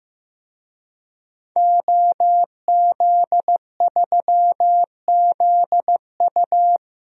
Morse code, heard as a keyed tone, about 16 wpm, 700 Hz. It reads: OZ3ZU
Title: DXCC-100 Call Signs - 5 Characters - 15wpm